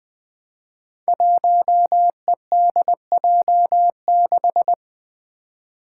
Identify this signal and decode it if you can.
Morse code, heard as a keyed tone, about 20 wpm, 700 Hz. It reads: 1EDJ6
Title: Five Challenging Letter-Number Combinations 20wpm